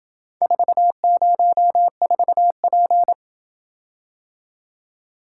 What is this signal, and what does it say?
Morse code, heard as a keyed tone, about 26 wpm, 700 Hz. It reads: 404P